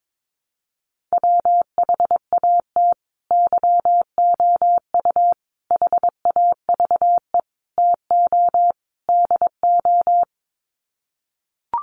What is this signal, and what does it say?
Morse code, heard as a keyed tone, about 22 wpm, 700 Hz. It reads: WHAT YOU HAVE TO DO
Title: Sets of 5 Words Encore 22wpm